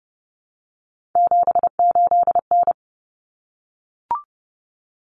Morse code, heard as a keyed tone, about 30 wpm, 700 Hz. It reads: 78N
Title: Common Cut Numbers - 3-Digit 30wpm